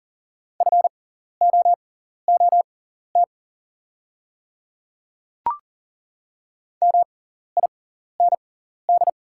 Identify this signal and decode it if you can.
Morse code, heard as a keyed tone, about 40 wpm, spaced as for 14 wpm, 700 Hz. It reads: FOOT MIND
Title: 6x Character Spacing - Top 500 Words - 40wpm